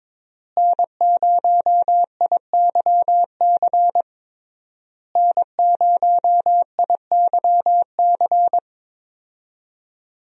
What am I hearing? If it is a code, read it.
Morse code, heard as a keyed tone, about 22 wpm, 700 Hz. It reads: N0IYC N0IYC
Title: USA Call Signs - 5 Characters - Repeated 3x - 22wpm